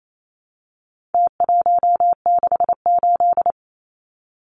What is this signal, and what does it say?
Morse code, heard as a keyed tone, about 28 wpm, 700 Hz. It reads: T168
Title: Numbers - Contest Style - 4-Digit 28wpm